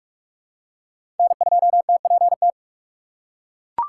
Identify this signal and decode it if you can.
Morse code, heard as a keyed tone, about 42 wpm, 700 Hz. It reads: N2TPT
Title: USA Call Signs - 5 Characters - 6x Character Spacing - 45wpm